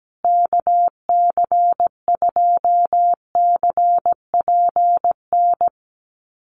Morse code, heard as a keyed tone, about 17 wpm, 700 Hz. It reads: KC2CPN
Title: USA Call Signs Repeated 3x 17wpm